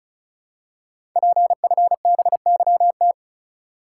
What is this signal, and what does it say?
Morse code, heard as a keyed tone, about 36 wpm, 690 Hz. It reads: PFBYT
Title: Code Groups - 5 Letters - 35wpm